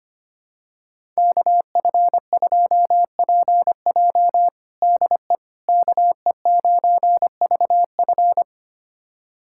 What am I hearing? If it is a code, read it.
Morse code, heard as a keyed tone, about 25 wpm, 700 Hz. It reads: KF2PJ DE KE9VF
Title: USA Callsign DE Callsign - 5 Characters - 25wpm